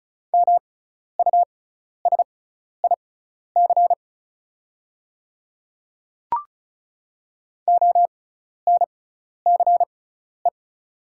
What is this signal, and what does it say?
Morse code, heard as a keyed tone, about 34 wpm, spaced as for 12 wpm, 700 Hz. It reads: MUSIC ONCE
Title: 6x Character Spacing - Top 300 Words - 35wpm